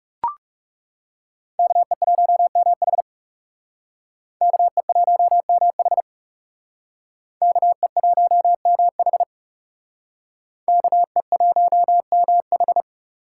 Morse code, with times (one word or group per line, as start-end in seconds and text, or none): 1.6-3.0 KE1MH
4.4-6.0 KE1MH
7.4-9.2 KE1MH
10.7-12.8 KE1MH